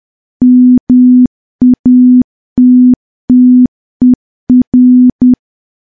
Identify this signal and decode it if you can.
Morse code, heard as a keyed tone, about 10 wpm, 250 Hz. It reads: MATTER
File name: morse / morse-training-10WPM